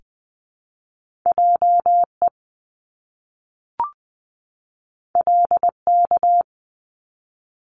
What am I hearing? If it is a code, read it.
Morse code, heard as a keyed tone, about 20 wpm, 700 Hz. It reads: JE LK
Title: Top Two Letter Combinations in Words 20wpm